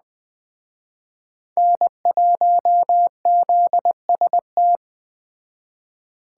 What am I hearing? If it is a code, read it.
Morse code, heard as a keyed tone, about 21 wpm, 700 Hz. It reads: N1ZST
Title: Speed Racing - USA 1x3 Call Signs 17wpm